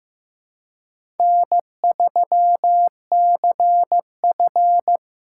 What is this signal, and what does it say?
Morse code, heard as a keyed tone, about 15 wpm, 700 Hz. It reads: N3CF
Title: USA Call Signs - 4 Characters - 2x Character Spacing - 15wpm